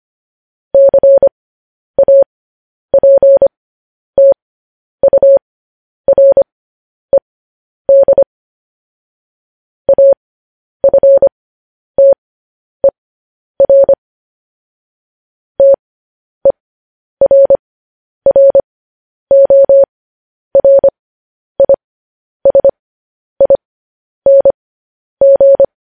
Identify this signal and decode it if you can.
Morse code, heard as a keyed tone, about 25 wpm, spaced as for 10 wpm, 550 Hz. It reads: CAPTURED AFTER TERRORISING
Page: news headlines in morse code at 10 wpm